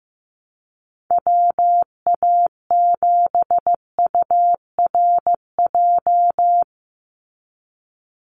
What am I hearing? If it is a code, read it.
Morse code, heard as a keyed tone, about 15 wpm, 700 Hz. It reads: WA7URJ
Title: USA Call Signs 15wpm